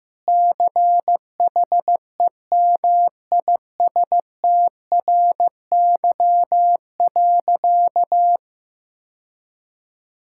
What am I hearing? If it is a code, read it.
Morse code, heard as a keyed tone, about 15 wpm, 700 Hz. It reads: CHEMISTRY.
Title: Sentences from Top 1000 Words - Review Part 6 - 15wpm